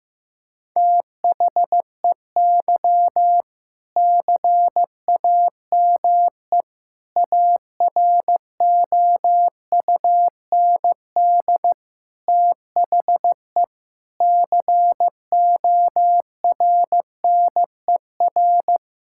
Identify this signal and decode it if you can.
Morse code, heard as a keyed tone, about 16 wpm, 700 Hz. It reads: THEY CAME AROUND THE CORNER
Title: Sentences from Top 1600 Words 15wpm